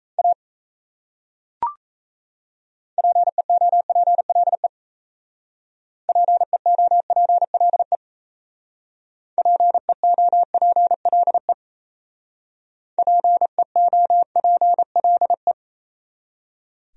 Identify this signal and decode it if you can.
Morse code, heard as a keyed tone, about 36 wpm, 700 Hz. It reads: A PEOPLE PEOPLE PEOPLE PEOPLE